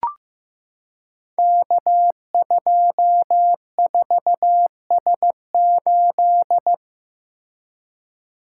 Morse code, heard as a keyed tone, about 15 wpm, 700 Hz. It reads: K24S8
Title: Code Groups - 5 Letter:Numbers - 15wpm